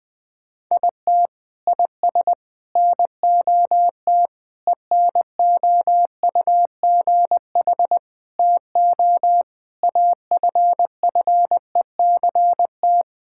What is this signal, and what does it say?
Morse code, heard as a keyed tone, about 21 wpm, 700 Hz. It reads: IT IS NOT ENOUGH TO AFFECT
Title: Sentences from Top 1000 Words - Review Part 6 - 20wpm